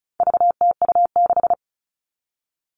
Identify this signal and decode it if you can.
Morse code, heard as a keyed tone, about 34 wpm, 700 Hz. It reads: VTU6
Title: All Cut Numbers - 4-Digit 35wpm